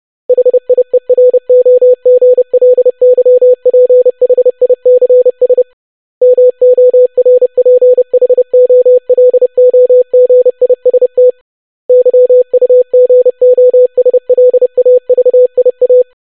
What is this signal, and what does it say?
Morse code, heard as a keyed tone, about 30 wpm, 500 Hz. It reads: HIEROGLYPHICS MORPHOLOGIST YUGOSLAVIA
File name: longwords-25wpm